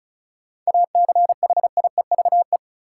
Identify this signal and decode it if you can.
Morse code, heard as a keyed tone, about 36 wpm, 700 Hz. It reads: ACHIEVE
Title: Sentences from Top 1000 Words - Review Part 5 - New Word Drill - 35wpm